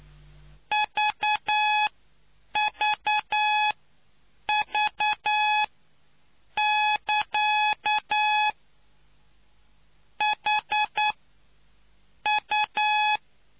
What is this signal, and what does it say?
Morse code, heard as a keyed tone, about 9 wpm, spaced as for 6 wpm, 850 Hz. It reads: VVV <KA> HU